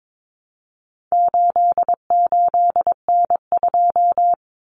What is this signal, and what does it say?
Morse code, heard as a keyed tone, about 22 wpm, 700 Hz. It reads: 88N2
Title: Common Cut Numbers - 4-Digit 22wpm